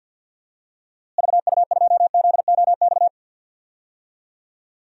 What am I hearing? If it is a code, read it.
Morse code, heard as a keyed tone, about 53 wpm, 700 Hz. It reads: VU2ZOX
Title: DXCC-100 Call Signs - 6 Characters - Repeated 2x - 50wpm